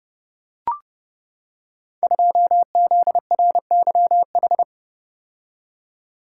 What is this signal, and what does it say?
Morse code, heard as a keyed tone, about 28 wpm, 700 Hz. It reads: 2ZRYH
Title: Code Groups - 5 Letter:Numbers - 30wpm